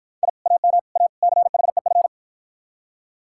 Morse code, heard as a keyed tone, about 53 wpm, 700 Hz. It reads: I AM A CHEF